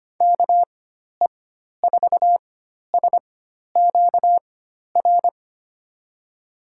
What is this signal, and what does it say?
Morse code, heard as a keyed tone, about 25 wpm, spaced as for 12 wpm, 700 Hz. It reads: KE4SQR
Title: USA Call Signs - 6 Characters - 4x Character Spacing - 25wpm